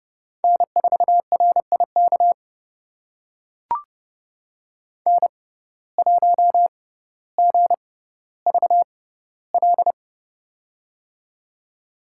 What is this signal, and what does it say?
Morse code, heard as a keyed tone, about 30 wpm, spaced as for 15 wpm, 700 Hz. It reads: N4RIK N1GVL
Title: USA Call Signs - 5 Characters - 6x Character Spacing - 30wpm